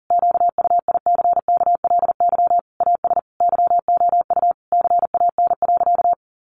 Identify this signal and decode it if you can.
Morse code, heard as a keyed tone, about 40 wpm, 700 Hz. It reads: QUICKLY AS YOU CAN.
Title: Sentences from Top 1100 Words 40wpm